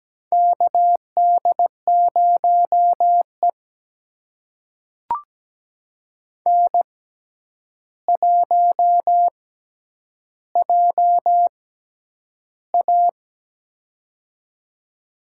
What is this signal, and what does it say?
Morse code, heard as a keyed tone, about 17 wpm, spaced as for 9 wpm, 700 Hz. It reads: KD0E N1JA
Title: USA Call Signs - 4 Characters - 6x Character Spacing - 17wpm